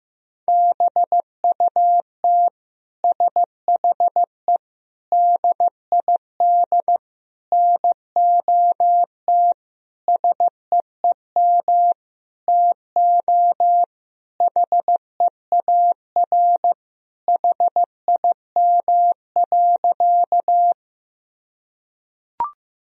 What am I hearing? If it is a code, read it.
Morse code, heard as a keyed tone, about 15 wpm, 700 Hz. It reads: BUT SHE DID NOT SEEM TO HEAR HIM.
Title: Sentences from Top 1000 Words - Review Part 1 - 15wpm